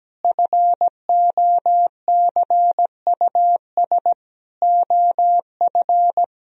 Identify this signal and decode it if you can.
Morse code, heard as a keyed tone, about 17 wpm, 700 Hz. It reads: FOCUS OF
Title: Sentences from Top 1000 Words - Review Part 4 - 17wpm